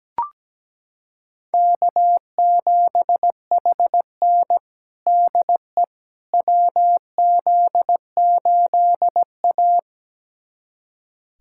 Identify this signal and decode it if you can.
Morse code, heard as a keyed tone, about 17 wpm, 700 Hz. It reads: K7HN DE WZ8A
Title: USA Callsign DE Callsign - 4 Characters - 17wpm